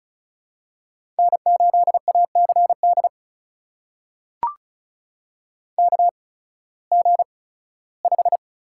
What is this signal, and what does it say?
Morse code, heard as a keyed tone, about 36 wpm, spaced as for 19 wpm, 700 Hz. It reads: N8ACD KG5
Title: USA Call Signs - 5 Characters - 8x Character Spacing - 35wpm